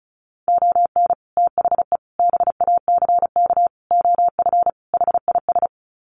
Morse code, heard as a keyed tone, about 34 wpm, 700 Hz. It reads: ON THE BACK OF HIS
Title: Sets of 5 Words Encore 35wpm